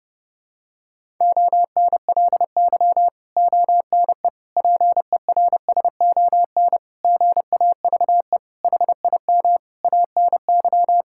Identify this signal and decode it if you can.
Morse code, heard as a keyed tone, about 30 wpm, 700 Hz. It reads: ONLY ONE PERSON GAVE HIM ANY
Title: Sentences from Top 1000 Words - Review Part 6 - 30wpm